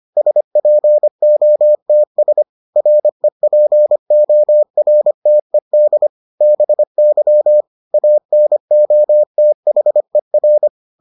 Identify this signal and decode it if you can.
Morse code, heard as a keyed tone, about 25 wpm, 600 Hz. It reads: SPOTS REPORTED BY ANOTHER